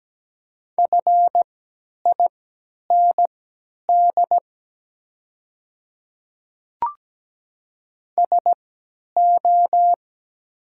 Morse code, heard as a keyed tone, about 17 wpm, spaced as for 10 wpm, 700 Hz. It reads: FIND SO
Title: 3x Character Spacing - Top 200 Words - 17wpm